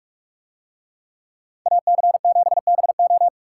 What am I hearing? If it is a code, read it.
Morse code, heard as a keyed tone, about 46 wpm, 700 Hz. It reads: AC7BO